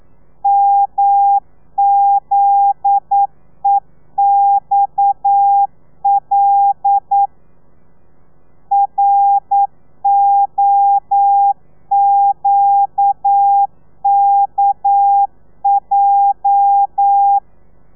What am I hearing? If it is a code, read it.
Morse code, heard as a keyed tone, about 9 wpm, 800 Hz. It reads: MZEXL ROQKJ
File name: telegrafia / i2tfj / Lez9 40